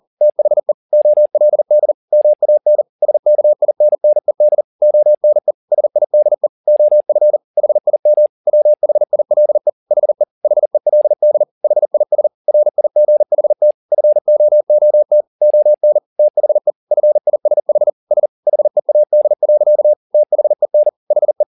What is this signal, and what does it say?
Morse code, heard as a keyed tone, about 40 wpm, 600 Hz. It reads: THE OLD MAN SKINNED ONE SIDE OF HIM WHILE HE HELD HIS RIGHT FOOT ON THE FISH S HEAD. THEN HE
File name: the old man and the sea 2 40wpm